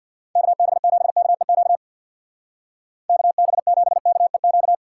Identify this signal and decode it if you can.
Morse code, heard as a keyed tone, about 59 wpm, 700 Hz. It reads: KB7KEQ KB7KEQ
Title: Speed Racing - USA 2x3 Call Signs 50wpm